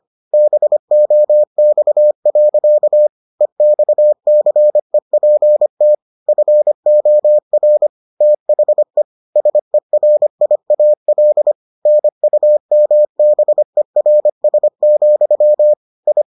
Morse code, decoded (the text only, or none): BOX. EXCEPT FOR THE SERIAL NUMBERS, I